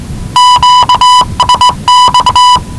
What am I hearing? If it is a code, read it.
Morse code, heard as a keyed tone, about 19 wpm, 990 Hz. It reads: QSX
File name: WellingtonRadioZlwCloseDownOnHfAndMf / 01-Zlw 4277khz 417.5 qrt until 0925z